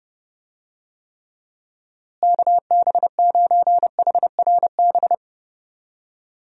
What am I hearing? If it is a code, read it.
Morse code, heard as a keyed tone, about 30 wpm, 700 Hz. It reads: KB9HRB
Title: USA Call Signs - 6 Characters - Repeated 2x - 30wpm